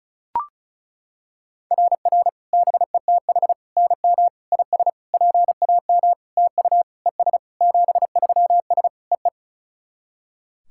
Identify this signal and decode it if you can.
Morse code, heard as a keyed tone, about 36 wpm, 690 Hz. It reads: RR BETH NM IS PAM TU ES 73S EE
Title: SOTA QSO Practice 35wpm